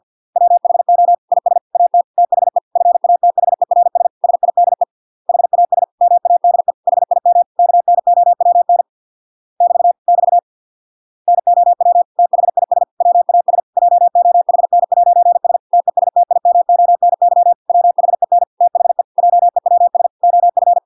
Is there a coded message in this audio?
Yes: Morse code from 0.4 to 20.8 s, heard as a keyed tone, about 50 wpm, 700 Hz. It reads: WHO IS AT THE FATHERS SIDE HAS MADE HIM KNOWN <BT> <BT> NOW THIS WAS JOHN'S TESTIMONY WHEN THE JEWS OF